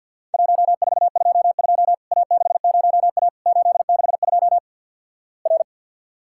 Morse code, heard as a keyed tone, about 50 wpm, 700 Hz. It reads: 1422 A60A 862 R
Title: Worked All Europe DX Contest - QTC Exchange - 50wpm